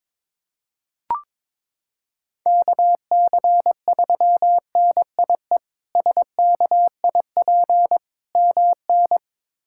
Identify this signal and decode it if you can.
Morse code, heard as a keyed tone, about 22 wpm, 700 Hz. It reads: KC3NIE SKIP MN
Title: K1USN SST QSO Practice - 22wpm